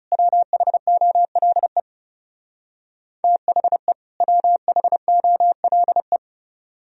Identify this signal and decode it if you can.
Morse code, heard as a keyed tone, about 36 wpm, 700 Hz. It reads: WHOLE THE WHOLE
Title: Speed-Racing - Sets of 2 Words 30wpm